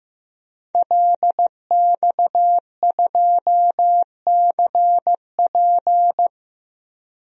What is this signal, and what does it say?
Morse code, heard as a keyed tone, about 15 wpm, 700 Hz. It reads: LX2CP
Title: DXCC-100 Call Signs - 5 Characters - Repeated 3x - 15wpm